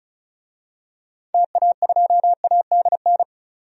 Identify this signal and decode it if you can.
Morse code, heard as a keyed tone, about 36 wpm, 700 Hz. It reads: TA2ADN
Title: DXCC-100 Call Signs - 6 Characters - 35wpm